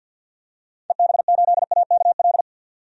Morse code, heard as a keyed tone, about 50 wpm, 700 Hz. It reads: EB8AKL
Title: DXCC-100 Call Signs - 6 Characters - 50wpm